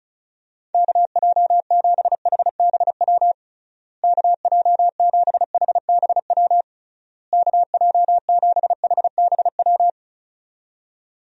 Morse code, heard as a keyed tone, about 36 wpm, 700 Hz. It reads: KJ7HBW KJ7HBW KJ7HBW
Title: USA Call Signs - 6 Characters - Repeated 3x - 35wpm